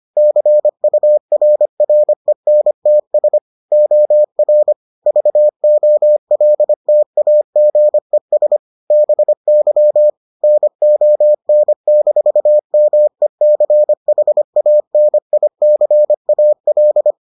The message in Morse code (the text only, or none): CURRENTS OR VOLTAGES BY NON-MECHANICAL